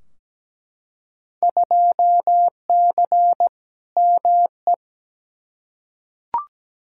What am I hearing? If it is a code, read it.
Morse code, heard as a keyed tone, about 17 wpm, 700 Hz. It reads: 2C ME
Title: Field Day Exchanges Encore - 17wpm